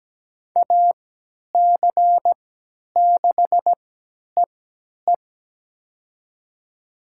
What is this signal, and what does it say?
Morse code, heard as a keyed tone, about 17 wpm, spaced as for 10 wpm, 700 Hz. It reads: AC6EE
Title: USA Call Signs - 5 Characters - 3x Character Spacing - 17wpm